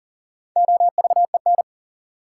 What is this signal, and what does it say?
Morse code, heard as a keyed tone, about 40 wpm, 700 Hz. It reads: OVEN
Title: Sentences from Top 1200 Words - New Word Drill Alt - 40wpm